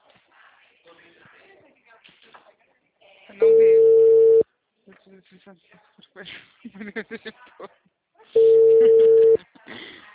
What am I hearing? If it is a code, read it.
Ringback tone (China, 450 Hz), with speech.